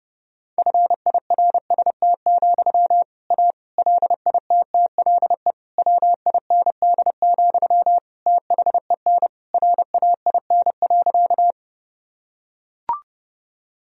Morse code, heard as a keyed tone, about 32 wpm, 700 Hz. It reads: FIRST, A LITTLE WIND, THEN RAIN.
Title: Sentences from Top 1000 Words - Review Part 2 - 30wpm